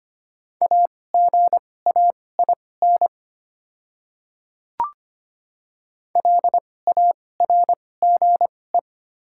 Morse code, heard as a keyed tone, about 25 wpm, spaced as for 18 wpm, 690 Hz. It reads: AGAIN LARGE
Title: 2x Character Spacing - Top 200 Words - 25wpm